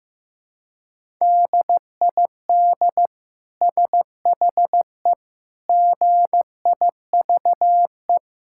Morse code, heard as a keyed tone, about 15 wpm, 700 Hz. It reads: DID SHE GIVE
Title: Sentences from Top 1000 Words - Review Part 5 - 15wpm